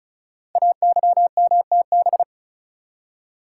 Morse code, heard as a keyed tone, about 36 wpm, 700 Hz. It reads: AYMTB
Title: Code Groups - 5 Letters - 35wpm